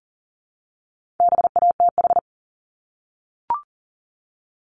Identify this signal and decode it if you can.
Morse code, heard as a keyed tone, about 41 wpm, 700 Hz. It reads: BATH
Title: Multi-letter Phonemes Repeated 3x 40wpm